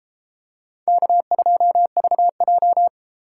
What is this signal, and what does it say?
Morse code, heard as a keyed tone, about 32 wpm, 700 Hz. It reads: K2VJ